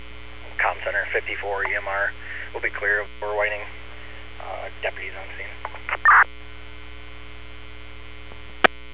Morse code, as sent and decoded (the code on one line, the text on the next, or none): none